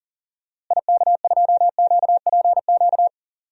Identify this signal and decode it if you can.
Morse code, heard as a keyed tone, about 40 wpm, 700 Hz. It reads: IK2QPQ
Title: DXCC-100 Call Signs - 6 Characters - Repeated 2x - 40wpm